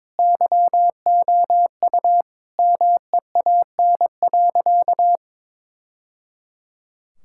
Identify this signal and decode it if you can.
Morse code, heard as a keyed tone, about 21 wpm, 700 Hz. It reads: YOU MEAN.
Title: Sentences from Top 1000 Words - Review Part 6 - 22wpm